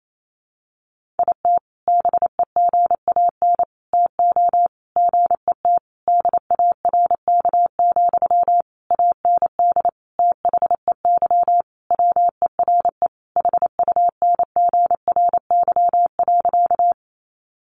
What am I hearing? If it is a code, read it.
Morse code, heard as a keyed tone, about 28 wpm, 700 Hz. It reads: IT BEGAN TO GET DARK, AND THEY WERE HUNGRY.
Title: Sentences from Top 1400 Words 28wpm